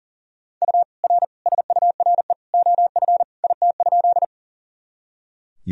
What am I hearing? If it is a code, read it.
Morse code, heard as a keyed tone, about 40 wpm, 700 Hz. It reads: U R SURE OF IT?
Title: Sentences from Top 300 Words and Abbreviations 40wpm